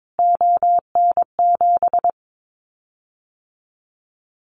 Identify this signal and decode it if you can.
Morse code, heard as a keyed tone, about 22 wpm, 700 Hz. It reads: ON7
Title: Callsign Prefixes for Top 25 DXCC - Repeated 3x - 22wpm